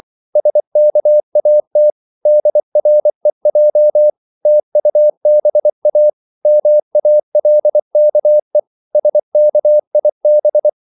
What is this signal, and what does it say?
Morse code, heard as a keyed tone, about 24 wpm, 600 Hz. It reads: SKAT DREJ TUBA MALKE SKIB
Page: Korte danske ord 24wpm